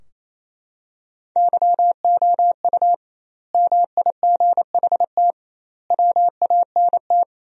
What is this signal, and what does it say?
Morse code, heard as a keyed tone, about 28 wpm, 700 Hz. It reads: YOU MIGHT WANT
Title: 2x Word-Spacing - Sentences from Top 300 Words - 28wpm